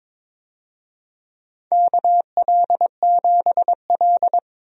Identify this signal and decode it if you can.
Morse code, heard as a keyed tone, about 22 wpm, 700 Hz. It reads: KL7L
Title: DXCC-100 Call Signs - 4 Characters - 22wpm